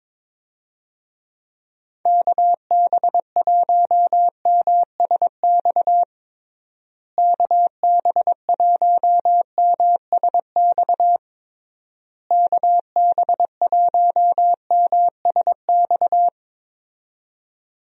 Morse code, heard as a keyed tone, about 22 wpm, 700 Hz. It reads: KB1MSX KB1MSX KB1MSX
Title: USA Call Signs - 6 Characters - Repeated 3x - 22wpm